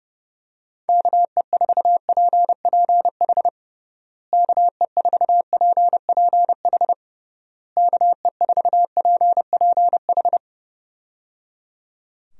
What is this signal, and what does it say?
Morse code, heard as a keyed tone, about 30 wpm, 700 Hz. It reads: KE4PPH KE4PPH KE4PPH
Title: USA Call Signs - 6 Characters - Repeated 3x - 30wpm